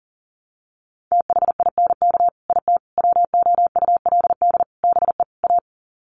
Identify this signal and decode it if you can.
Morse code, heard as a keyed tone, about 40 wpm, 700 Hz. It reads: THINK IT WOULD BE A
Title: Sets of 5 Words Encore2 40wpm